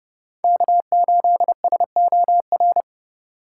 Morse code, heard as a keyed tone, about 32 wpm, 700 Hz. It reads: K8SOR
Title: USA Call Signs - 5 Characters - 4x Character Spacing - 30wpm